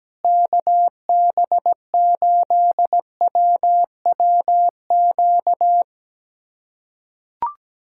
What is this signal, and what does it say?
Morse code, heard as a keyed tone, about 17 wpm, 700 Hz. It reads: KB8WWQ
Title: USA Callsign DE Callsign - 6 Characters - 17wpm